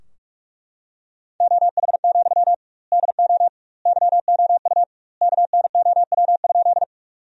Morse code, heard as a keyed tone, about 41 wpm, 700 Hz. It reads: OH, DO YOU KNOW?
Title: 2x Word-Spacing - Sentences from Top 500 Words - 45wpm